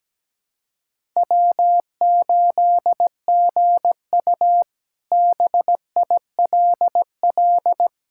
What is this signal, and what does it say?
Morse code, heard as a keyed tone, about 17 wpm, 700 Hz. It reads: W8GU BILL
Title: K1USN SST QSO Practice - 17wpm